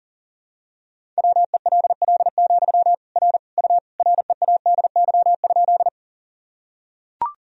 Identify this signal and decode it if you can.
Morse code, heard as a keyed tone, about 40 wpm, 700 Hz. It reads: WELL, R U READY?
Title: Sentences from Top 400 Words and Abbreviations 40wpm